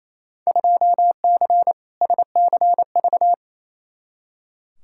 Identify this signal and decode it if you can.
Morse code, heard as a keyed tone, about 28 wpm, 700 Hz. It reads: 2C SCV
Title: Field Day Exchanges Encore - 28wpm